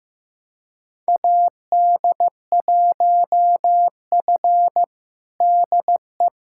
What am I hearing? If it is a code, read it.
Morse code, heard as a keyed tone, about 15 wpm, 700 Hz. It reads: AD1F DE
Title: USA Callsign DE Callsign - 4 Characters - 15wpm